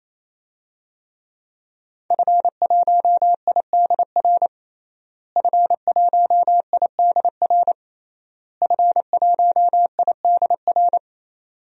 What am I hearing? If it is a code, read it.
Morse code, heard as a keyed tone, about 28 wpm, 700 Hz. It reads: F1IDR F1IDR F1IDR
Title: DXCC-100 Call Signs - 5 Characters - Repeated 3x - 28wpm